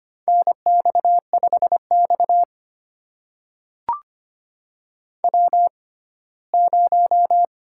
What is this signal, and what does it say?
Morse code, heard as a keyed tone, about 25 wpm, spaced as for 17 wpm, 700 Hz. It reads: NX5X W0
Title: USA Call Signs - 4 Characters - 6x Character Spacing - 25wpm